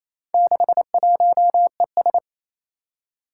Morse code, heard as a keyed tone, about 27 wpm, 700 Hz. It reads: 61ES
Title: Four Challenging Letter-Number Combinations 28wpm